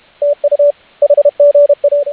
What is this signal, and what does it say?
Morse code, heard as a keyed tone, about 32 wpm, 590 Hz, with radio noise. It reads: TU HGA